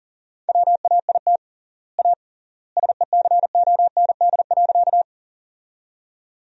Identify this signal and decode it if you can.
Morse code, heard as a keyed tone, about 40 wpm, 700 Hz. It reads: WAIT A SECOND.
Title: 3x Word-Spacing - Sentences from Top 500 Words - 40wpm